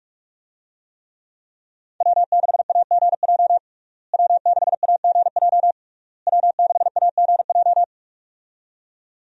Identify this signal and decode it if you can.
Morse code, heard as a keyed tone, about 41 wpm, 700 Hz. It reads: W6AGJ W6AGJ W6AGJ
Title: USA Call Signs - 5 Characters - Repeated 3x - 45wpm